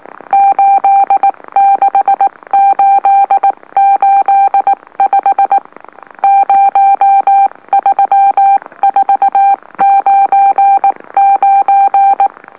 Morse code, heard as a keyed tone, about 19 wpm, spaced as for 17 wpm, 790 Hz. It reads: 86885 03499
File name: M04- -2345utc-LOLO